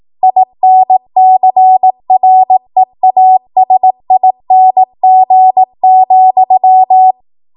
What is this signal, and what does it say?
Morse code, heard as a keyed tone, about 18 wpm, 750 Hz. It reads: INCREASING,